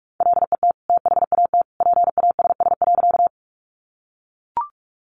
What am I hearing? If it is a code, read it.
Morse code, heard as a keyed tone, about 44 wpm, 700 Hz. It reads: LET THAT PASS.
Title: Sentences from Top 400 Words 45wpm